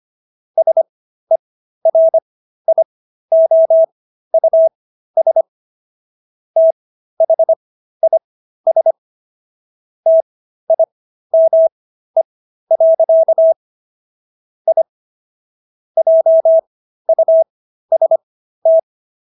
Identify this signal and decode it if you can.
Morse code, heard as a keyed tone, about 25 wpm, spaced as for 13 wpm, 650 Hz. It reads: SERIOUS THIS TIME. I JUST
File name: SS106 13